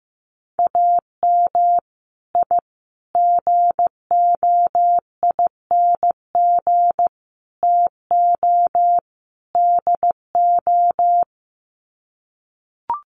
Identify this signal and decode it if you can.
Morse code, heard as a keyed tone, about 15 wpm, 700 Hz. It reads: AM I GOING TO DO
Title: Sets of 5 Words Encore 15wpm